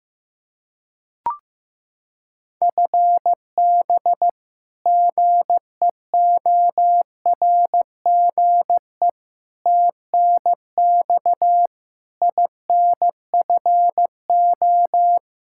Morse code, heard as a keyed tone, about 15 wpm, 700 Hz. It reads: FB GEORGE TNX INFO
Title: K1USN SST QSO Practice 15wpm